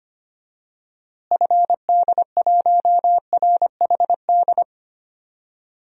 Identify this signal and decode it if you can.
Morse code, heard as a keyed tone, about 25 wpm, 700 Hz. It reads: FD1RHD
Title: DXCC-100 Call Signs - 6 Characters - Repeated 2x - 25wpm